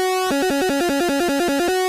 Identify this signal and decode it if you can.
There is no code.